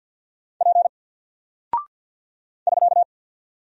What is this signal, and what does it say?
Morse code, heard as a keyed tone, about 48 wpm, 700 Hz. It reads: P <SK>
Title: Single Character (Letter:Number:Punctuation:Prosign) Alt 50wpm